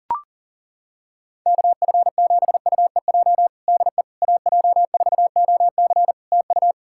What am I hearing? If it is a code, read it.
Morse code, heard as a keyed tone, about 40 wpm, 700 Hz. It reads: KF7UEJ DE AJ4OC TU
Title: Practice QSO - Exchange 4 - Ending - 40wpm